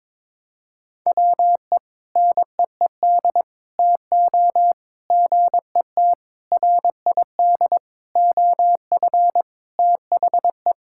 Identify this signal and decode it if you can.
Morse code, heard as a keyed tone, about 22 wpm, 700 Hz. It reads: WE NEED TO GET RID OF THE
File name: practice / 22wpm / Sentences with 4-Word Phrases - 22wpm